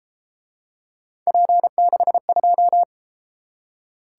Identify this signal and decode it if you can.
Morse code, heard as a keyed tone, about 32 wpm, 700 Hz. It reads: P62